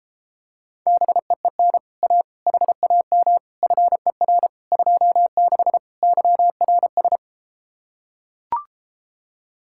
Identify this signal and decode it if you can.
Morse code, heard as a keyed tone, about 33 wpm, 700 Hz. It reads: BEEN A HAM FER 26 YRS